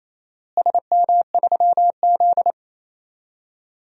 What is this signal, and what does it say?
Morse code, heard as a keyed tone, about 28 wpm, 700 Hz. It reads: SM3Z
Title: DXCC-100 Call Signs - 4 Characters - Repeated 2x - 28wpm